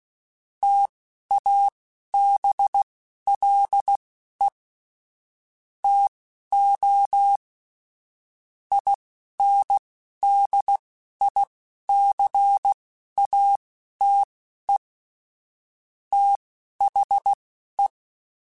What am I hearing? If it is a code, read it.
Morse code, heard as a keyed tone, about 16 wpm, spaced as for 11 wpm, 800 Hz. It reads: TABLE TO INDICATE THE
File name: Technical Subjects / Technical Subjects-mp3s / Gates Intro @10